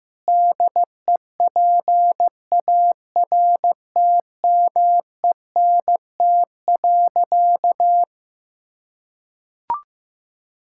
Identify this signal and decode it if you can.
Morse code, heard as a keyed tone, about 15 wpm, 700 Hz. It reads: DEPARTMENT.
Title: Sentences from Top 1000 Words - Review Part 5 - 15wpm